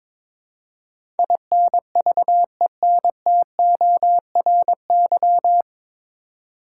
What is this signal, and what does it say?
Morse code, heard as a keyed tone, about 22 wpm, 700 Hz. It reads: INVENTORY
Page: Morse Code Ninja practice 22wpm